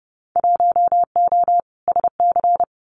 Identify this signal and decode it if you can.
Morse code, heard as a keyed tone, about 30 wpm, 700 Hz. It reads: 1O SC
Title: Winter Field Day Exchanges 30wpm